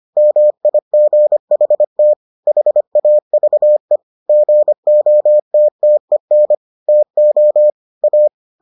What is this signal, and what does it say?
Morse code, heard as a keyed tone, about 25 wpm, 600 Hz. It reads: MIGHT HAVE GOTTEN TO A